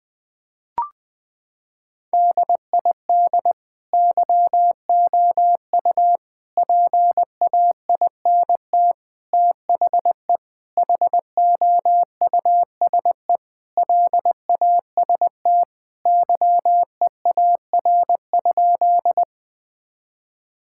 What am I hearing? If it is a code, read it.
Morse code, heard as a keyed tone, about 21 wpm, 700 Hz. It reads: DID YOU PAINT THE HOUSE LAST YEAR?
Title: Sentences from Top 1000 Words - Review Part 6 - 20wpm